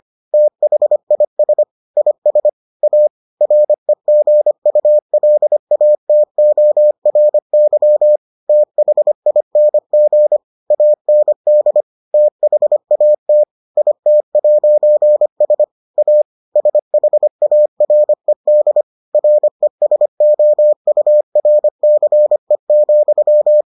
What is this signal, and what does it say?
Morse code, heard as a keyed tone, about 25 wpm, 600 Hz. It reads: THIS IS A REGULATORY THING AND THAT IT'S A SHARED RESOURCE,